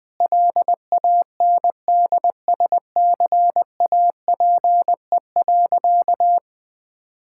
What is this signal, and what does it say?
Morse code, heard as a keyed tone, about 20 wpm, 700 Hz. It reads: LANDSCAPE.
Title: Difficult Sentences with Hints - 20wpm